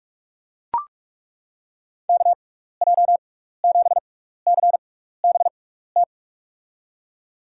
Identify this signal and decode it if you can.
Morse code, heard as a keyed tone, about 41 wpm, spaced as for 15 wpm, 700 Hz. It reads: KJ7CBT
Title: USA Call Signs - 6x Character Spacing - 45wpm